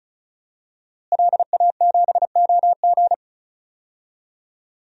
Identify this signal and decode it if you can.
Morse code, heard as a keyed tone, about 36 wpm, 700 Hz. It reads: LA7OG
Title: DXCC-100 Call Signs - 5 Characters - Repeated 3x - 35wpm